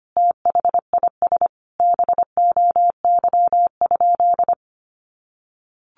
Morse code, heard as a keyed tone, about 25 wpm, 700 Hz. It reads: THIS BOY?
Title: Sentences from Top 1500 Words 25wpm